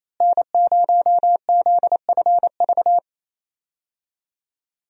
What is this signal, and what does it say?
Morse code, heard as a keyed tone, about 28 wpm, 700 Hz. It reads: N0ZFV
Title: USA Call Signs - 5 Characters - Repeated 2x - 28wpm